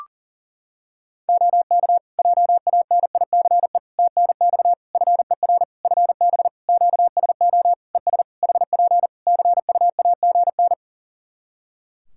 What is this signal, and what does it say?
Morse code, heard as a keyed tone, about 40 wpm, 700 Hz. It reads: OK JANICE TNX FER FB QSO ES HP CUAGN
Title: K1USN SST QSO Practice 40wpm